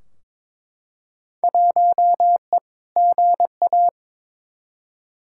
Morse code, heard as a keyed tone, about 22 wpm, 700 Hz. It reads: 1E GA
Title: Field Day Exchanges - 22wpm